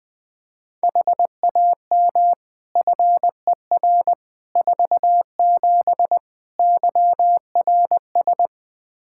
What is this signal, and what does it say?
Morse code, heard as a keyed tone, about 20 wpm, 700 Hz. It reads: HAM FER 47 YRS
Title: K1USN SST QSO Practice 20wpm